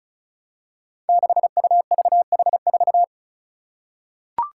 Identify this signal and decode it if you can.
Morse code, heard as a keyed tone, about 36 wpm, 700 Hz. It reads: 6UVH4
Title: Five Challenging Letter-Number Combinations 35wpm